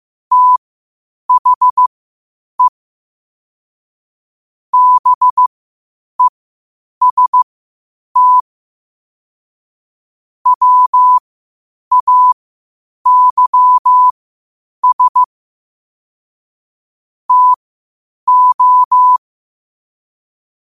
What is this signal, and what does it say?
Morse code, heard as a keyed tone, about 15 wpm, spaced as for 8 wpm, 1000 Hz. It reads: THE BEST WAYS TO
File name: qotdfiles / 8wpm